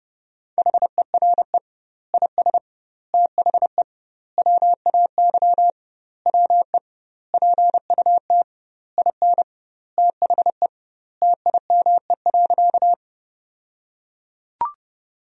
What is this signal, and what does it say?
Morse code, heard as a keyed tone, about 29 wpm, 700 Hz. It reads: HERE IS THE WAY WE PUT IN THE TIME.
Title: 2x Word-Spacing - Sentences from Top 200 Words - 30wpm